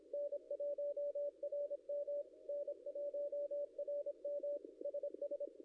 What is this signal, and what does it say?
Morse code, heard as a keyed tone, about 26 wpm, 560 Hz. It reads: N1RM N1RM SS
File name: contests / ss-cw-k5tr-op-n5rz -0700